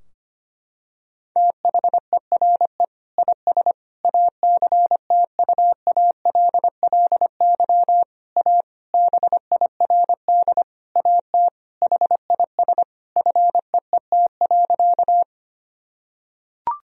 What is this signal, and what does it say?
Morse code, heard as a keyed tone, about 25 wpm, 700 Hz. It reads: THERE IS ACTUALLY A BIRD AT HIS FEET.
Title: Sentences from Top 1000 Words - Review Part 4 - 25wpm